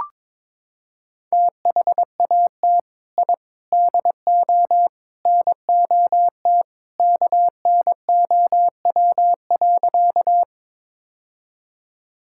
Morse code, heard as a keyed tone, about 21 wpm, 700 Hz. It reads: THAT I DO NOT KNOW.
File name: practice / 22wpm / Sentences with 4-Word Phrases - 22wpm